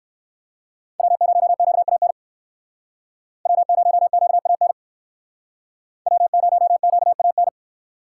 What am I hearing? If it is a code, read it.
Morse code, heard as a keyed tone, about 59 wpm, 700 Hz. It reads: W0QAN W0QAN W0QAN